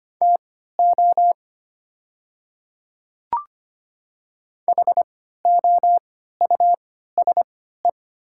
Morse code, heard as a keyed tone, about 25 wpm, spaced as for 14 wpm, 700 Hz. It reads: TO HOUSE
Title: 3x Character Spacing - Top 300 Words - 25wpm